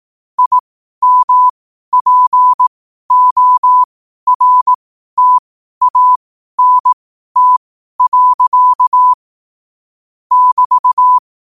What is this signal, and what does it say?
Morse code, heard as a keyed tone, about 18 wpm, spaced as for 12 wpm, 1000 Hz. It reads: IMPORTANT. <BT>
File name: qotdfiles / 12wpm